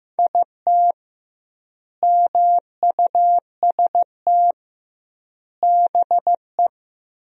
Morse code, heard as a keyed tone, about 15 wpm, 700 Hz. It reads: IT MUST BE
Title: 2x Word-Spacing - Sentences from Top 300 Words - 15wpm